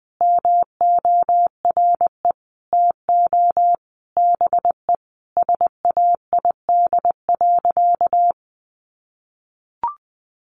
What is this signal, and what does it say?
Morse code, heard as a keyed tone, about 20 wpm, 700 Hz. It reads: MORE TO BE SAID.
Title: Sentences from Top 100 Words 20wpm